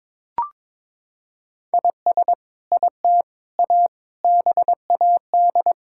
Morse code, heard as a keyed tone, about 22 wpm, 700 Hz. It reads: IS IT A BAD
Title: Sentences from Top 1000 Words - Review Part 5 - 22wpm